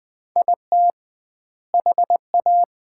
Morse code, heard as a keyed tone, about 21 wpm, 700 Hz. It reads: IT HA
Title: 2x Word-Spacing - Sets of 4 Words - 20wpm